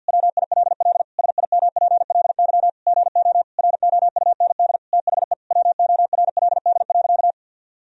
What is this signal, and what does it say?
Morse code, heard as a keyed tone, about 50 wpm, 700 Hz. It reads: WILL SIMPLY GO ROUND THE WORLD.
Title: Sentences from Top 1000 Words - Review Part 4 - 50wpm